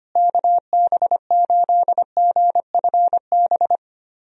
Morse code, heard as a keyed tone, about 25 wpm, 700 Hz. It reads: KB8GFB
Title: USA Call Signs - 6 Characters - Repeated 2x - 25wpm